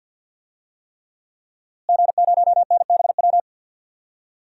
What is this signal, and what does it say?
Morse code, heard as a keyed tone, about 53 wpm, 700 Hz. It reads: G0NBW
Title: DXCC-100 Call Signs - 5 Characters - Repeated 3x - 50wpm